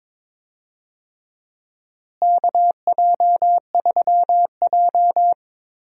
Morse code, heard as a keyed tone, about 22 wpm, 700 Hz. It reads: KJ3J
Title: USA Call Signs - 4 Characters - Repeated 3x - 22wpm